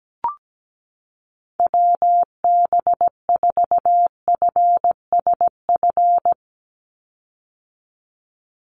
Morse code, heard as a keyed tone, about 17 wpm, 700 Hz. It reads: WB4FSF
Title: Call Signs - 17wpm